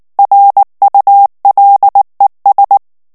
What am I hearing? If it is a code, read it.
Morse code, heard as a keyed tone, about 19 wpm, 800 Hz. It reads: RULES
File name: ANGLAIS texte clair 3 vitesse 18 mots minute